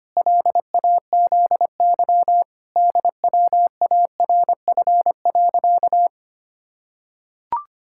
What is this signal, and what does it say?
Morse code, heard as a keyed tone, about 25 wpm, 700 Hz. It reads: LAZY DWARF.
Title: Pangram Challenge v2 - 25wpm